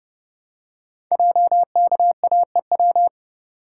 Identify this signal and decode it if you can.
Morse code, heard as a keyed tone, about 30 wpm, 700 Hz. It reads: JKAEW
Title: Code Groups - 5 Letters - 30wpm